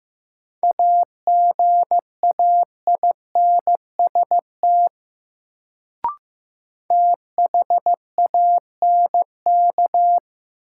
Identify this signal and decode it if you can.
Morse code, heard as a keyed tone, about 15 wpm, 700 Hz. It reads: AGAINST THANK
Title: Sets of 2 Words - New Word Drill - 15wpm